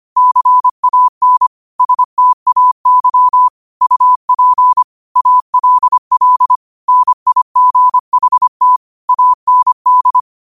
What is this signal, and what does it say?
Morse code, heard as a keyed tone, about 25 wpm, 1000 Hz. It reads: CAN STAY UP ALL NIGHT AND
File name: qotdfiles / 25wpm